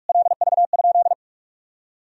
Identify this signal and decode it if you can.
Morse code, heard as a keyed tone, about 48 wpm, 700 Hz. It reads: LV?
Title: Sentences from Top 400 Words and Abbreviations 45wpm